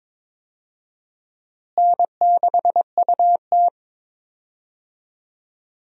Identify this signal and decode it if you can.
Morse code, heard as a keyed tone, about 22 wpm, 700 Hz. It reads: N6UT
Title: USA Call Signs - 4 Characters - Repeated 2x - 22wpm